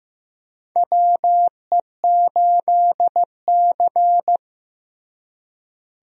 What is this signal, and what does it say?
Morse code, heard as a keyed tone, about 15 wpm, 700 Hz. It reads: WE8C
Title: USA Call Signs - 4 Characters - Repeated 2x - 15wpm